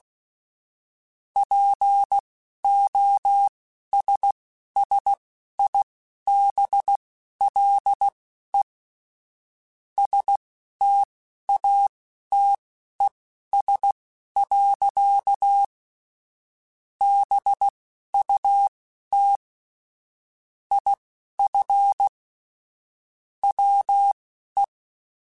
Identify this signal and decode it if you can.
Morse code, heard as a keyed tone, about 16 wpm, spaced as for 11 wpm, 800 Hz. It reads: POSSIBLE STATES. BUT IF WE
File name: Technical Subjects / Technical Subjects-mp3s / AND Gate @10